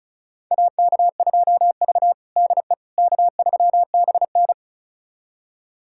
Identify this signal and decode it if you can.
Morse code, heard as a keyed tone, about 36 wpm, 700 Hz. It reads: AK2V DE K3BN
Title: USA Callsign DE Callsign - 4 Characters - 35wpm